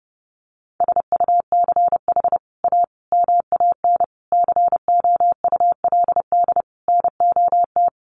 Morse code, heard as a keyed tone, about 28 wpm, 700 Hz. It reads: SUCH A MAN COULD NOT
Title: Sentences from Top 1100 Words 30wpm